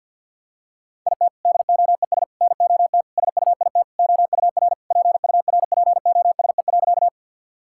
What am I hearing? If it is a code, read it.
Morse code, heard as a keyed tone, about 50 wpm, 700 Hz. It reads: IT DOES NOT SUIT OUR PURPOSE.
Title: Sentences from Top 1600 Words 50wpm